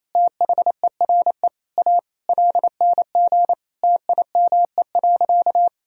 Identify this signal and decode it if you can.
Morse code, heard as a keyed tone, about 28 wpm, 700 Hz. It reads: THERE A LNG TIME.
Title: Sentences from Top 400 Words and Abbreviations 28wpm